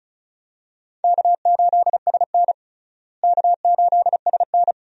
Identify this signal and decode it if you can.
Morse code, heard as a keyed tone, about 36 wpm, 700 Hz. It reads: K8SN K8SN
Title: USA Call Signs - 4 Characters - Repeated 2x - 35wpm